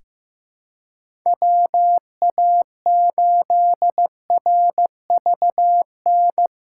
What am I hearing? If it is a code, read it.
Morse code, heard as a keyed tone, about 15 wpm, 700 Hz. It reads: WA8RVN
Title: USA Call Signs - 6 Characters - 3x Character Spacing - 15wpm